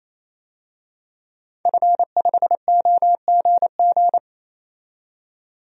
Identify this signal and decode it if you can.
Morse code, heard as a keyed tone, about 28 wpm, 700 Hz. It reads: F5OGG
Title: DXCC-100 Call Signs - 5 Characters - 28wpm